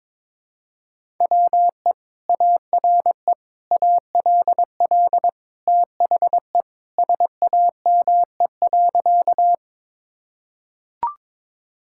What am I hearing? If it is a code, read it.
Morse code, heard as a keyed tone, about 22 wpm, 700 Hz. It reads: WE ARE ALL THE SAME.
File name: practice / 22wpm / Sentences with 2-Word Phrases - 22wpm